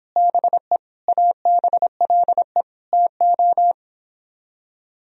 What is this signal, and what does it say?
Morse code, heard as a keyed tone, about 29 wpm, 700 Hz. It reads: BE ABLE TO